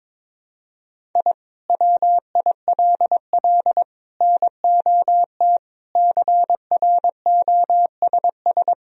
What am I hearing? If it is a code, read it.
Morse code, heard as a keyed tone, about 21 wpm, 700 Hz. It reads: I WILL NOT CROSS
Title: Sentences from Top 1000 Words - Review Part 1 - 22wpm